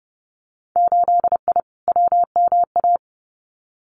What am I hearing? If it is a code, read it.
Morse code, heard as a keyed tone, about 32 wpm, 700 Hz. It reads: 8I WMA
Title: Winter Field Day Exchanges Encore 30wpm